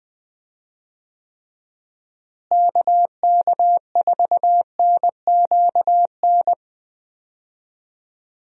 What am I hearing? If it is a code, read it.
Morse code, heard as a keyed tone, about 20 wpm, 700 Hz. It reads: KK4NQN
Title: USA Call Signs - 6 Characters - Repeated 3x - 20wpm